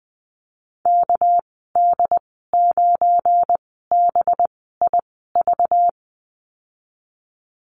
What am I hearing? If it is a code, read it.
Morse code, heard as a keyed tone, about 20 wpm, spaced as for 14 wpm, 700 Hz. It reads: KD9BIV
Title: 231 Lesson 37g - Letter x - Call Signs 20wpm